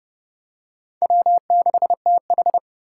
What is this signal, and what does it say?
Morse code, heard as a keyed tone, about 29 wpm, 700 Hz. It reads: W6TH